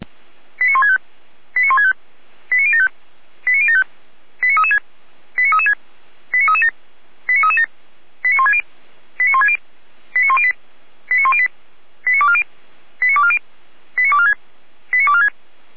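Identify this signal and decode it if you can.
Five-tone selective call (ZVEI), heard as quick runs of five tones, tones of about 70 ms. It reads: 89165 89165 89075 89075 89336 89336 89337 89337 89160 89160 89108 89108 89240 89240 89246 89246
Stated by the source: ZVEI five-tone call received by SDR.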